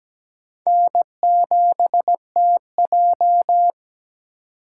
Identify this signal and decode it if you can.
Morse code, heard as a keyed tone, about 17 wpm, 700 Hz. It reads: N7TJ
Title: USA Call Signs - 4 Characters - 2x Character Spacing - 17wpm